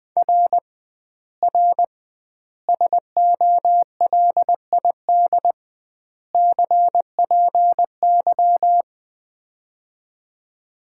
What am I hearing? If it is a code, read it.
Morse code, heard as a keyed tone, about 20 wpm, 700 Hz. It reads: R R SOLID CPY
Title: Common QSO Phrases - 2x Word-Spacing - 20wpm